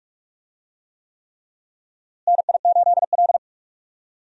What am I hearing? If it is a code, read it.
Morse code, heard as a keyed tone, about 46 wpm, 700 Hz. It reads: NI8L
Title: USA Call Signs - 4 Characters - Repeated 3x - 45wpm